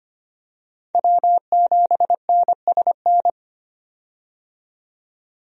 Morse code, heard as a keyed tone, about 25 wpm, 700 Hz. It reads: W7NSN
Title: USA Call Signs - 5 Characters - 25wpm